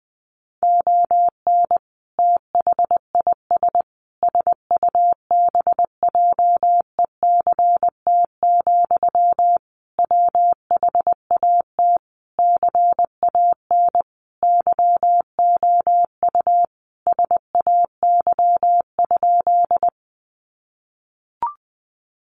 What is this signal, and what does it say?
Morse code, heard as a keyed tone, about 20 wpm, 700 Hz. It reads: ON THIS SUBJECT, WHAT CAN YOU SAY?
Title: Sentences from Top 800 Words 20wpm